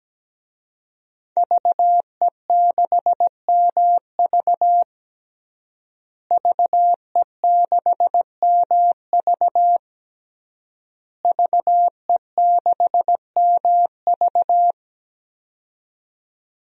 Morse code, heard as a keyed tone, about 17 wpm, 700 Hz. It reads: VE6MV VE6MV VE6MV
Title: DXCC-100 Call Signs - 5 Characters - Repeated 3x - 17wpm